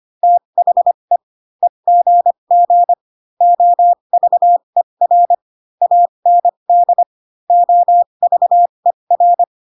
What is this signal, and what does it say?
Morse code, heard as a keyed tone, about 25 wpm, spaced as for 22 wpm, 700 Hz. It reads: THE EGG OVER AND OVER